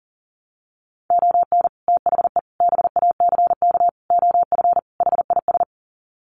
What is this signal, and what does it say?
Morse code, heard as a keyed tone, about 40 wpm, 700 Hz. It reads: ON THE BACK OF HIS
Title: Sets of 5 Words Encore 40wpm